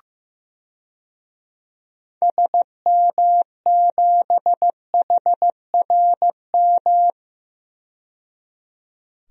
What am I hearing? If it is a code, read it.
Morse code, heard as a keyed tone, about 15 wpm, 700 Hz. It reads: SM7HRM
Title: DXCC-100 Call Signs - 6 Characters - 15wpm